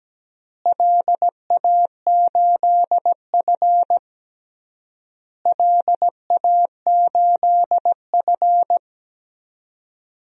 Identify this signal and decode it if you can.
Morse code, heard as a keyed tone, about 17 wpm, 700 Hz. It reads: LA8F LA8F
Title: DXCC-100 Call Signs - 4 Characters - Repeated 3x - 17wpm